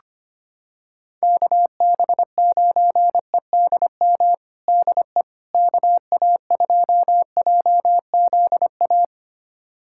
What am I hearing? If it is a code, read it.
Morse code, heard as a keyed tone, about 25 wpm, 700 Hz. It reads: KB9EDM DE KA2JZA
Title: USA Callsign DE Callsign 25wpm